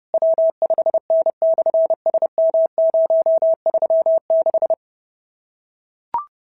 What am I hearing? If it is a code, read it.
Morse code, heard as a keyed tone, about 30 wpm, 650 Hz. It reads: W5N/SM036
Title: SOTA QSO Practice 30wpm